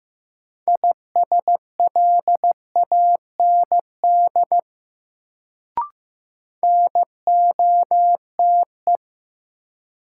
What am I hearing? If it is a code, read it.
Morse code, heard as a keyed tone, about 15 wpm, 700 Hz. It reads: ISLAND NOTE
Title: Sentences from Top 500 Words - New Word Drill - 15wpm